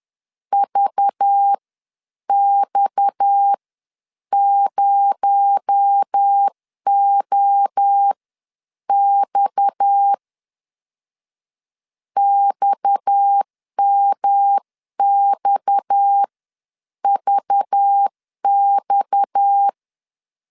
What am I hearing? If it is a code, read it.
Morse code, heard as a keyed tone, about 10 wpm, spaced as for 8 wpm, 800 Hz. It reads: VX0OX XMXVX
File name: CW / DET / 1 6 1500C S C 10wpm 4252